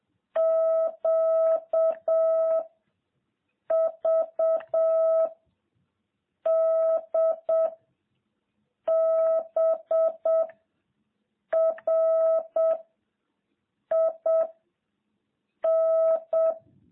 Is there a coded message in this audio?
Yes: Morse code from 0.4 to 16.5 s, heard as a keyed tone, about 7 wpm, 650 Hz. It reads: Q V D B R I N